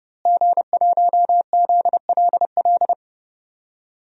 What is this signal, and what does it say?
Morse code, heard as a keyed tone, about 30 wpm, 700 Hz. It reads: G1ZLL
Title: DXCC-100 Call Signs - 5 Characters - Repeated 3x - 30wpm